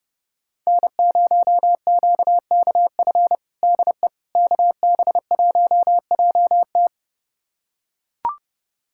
Morse code, heard as a keyed tone, about 30 wpm, 700 Hz. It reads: N0QKF DE KB1JT
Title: USA Callsign DE Callsign - 5 Characters - 30wpm